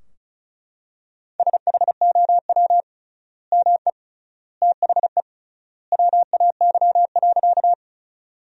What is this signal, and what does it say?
Morse code, heard as a keyed tone, about 36 wpm, 700 Hz. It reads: SHOW ME THE WAY.
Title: 3x Word-Spacing - Sentences from Top 200 Words - 35wpm